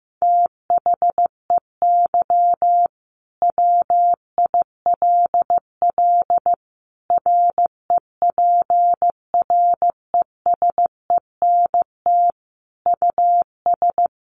Morse code, heard as a keyed tone, about 15 wpm, 700 Hz. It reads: THEY WILL REPRESENT US
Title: Sentences from Top 1200 Words 15wpm